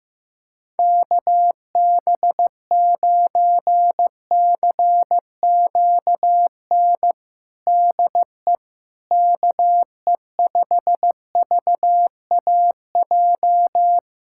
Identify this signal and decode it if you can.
Morse code, heard as a keyed tone, about 15 wpm, 700 Hz. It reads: KB9CQN DE KE5VAJ
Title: USA Callsign DE Callsign - 6 Characters - 15wpm